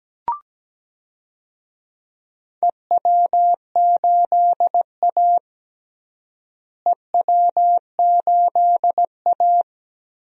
Morse code, heard as a keyed tone, about 17 wpm, 700 Hz. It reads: EW8A EW8A
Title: DXCC-100 Call Signs - 4 Characters - Repeated 2x - 17wpm